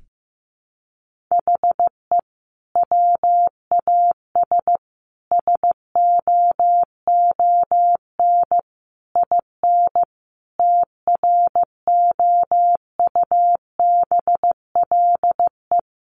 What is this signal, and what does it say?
Morse code, heard as a keyed tone, about 14 wpm, 700 Hz. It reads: HE WAS SOON IN TROUBLE
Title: Sentences from Top 1200 Words 15wpm